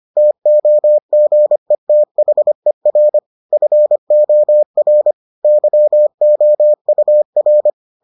Morse code, heard as a keyed tone, about 25 wpm, 600 Hz. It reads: TOGETHER FOR YOUR